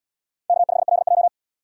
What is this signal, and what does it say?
Morse code, heard as a keyed tone, about 68 wpm, 700 Hz. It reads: K5LJ